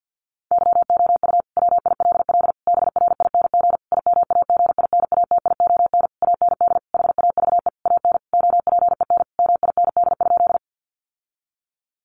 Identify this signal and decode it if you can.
Morse code, heard as a keyed tone, about 50 wpm, 700 Hz. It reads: YOU WILL BRING IMAGINATION AND HAVE AN OPEN MIND?
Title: Sentences from Top 1300 Words 50wpm